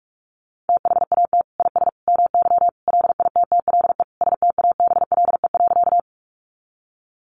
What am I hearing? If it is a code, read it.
Morse code, heard as a keyed tone, about 44 wpm, 700 Hz. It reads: THAT IS MY LITTLE STABLE.
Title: Sentences from Top 1500 Words 45wpm